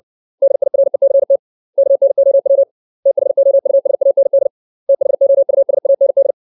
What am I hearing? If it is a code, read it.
Morse code, heard as a keyed tone, about 60 wpm, 550 Hz. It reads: DIRECT KNOW THOUSAND THOUSAND
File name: 632 words / 632 Words 60WPM